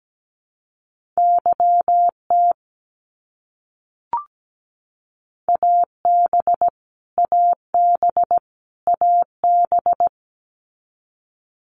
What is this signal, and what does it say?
Morse code, heard as a keyed tone, about 17 wpm, 700 Hz. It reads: YT AB AB AB
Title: Canadian Provinces and Territories - Repeated 3x - 17wpm